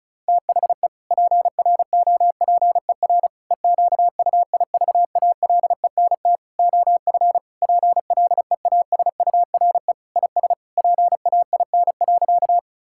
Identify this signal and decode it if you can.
Morse code, heard as a keyed tone, about 34 wpm, 700 Hz. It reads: THE PROPER EQUIVALENT OF PLEASURE IS PAIN.
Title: Sentences from Top 1600 Words 35wpm